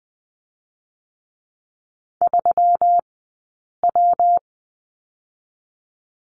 Morse code, heard as a keyed tone, about 20 wpm, spaced as for 8 wpm, 700 Hz. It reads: 3W
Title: 126 Lesson 21d - Number 6 - Two Characters 20wpm